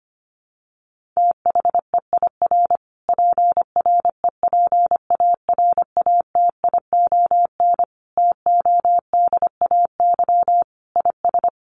Morse code, heard as a keyed tone, about 25 wpm, 700 Hz. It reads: THEIR PREPARATION TODAY IS
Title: Sentences from Top 1400 Words 25wpm